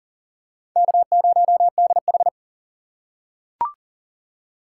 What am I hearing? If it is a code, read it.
Morse code, heard as a keyed tone, about 39 wpm, 700 Hz. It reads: K0DH
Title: USA Call Signs - 4 Characters - 2x Character Spacing - 40wpm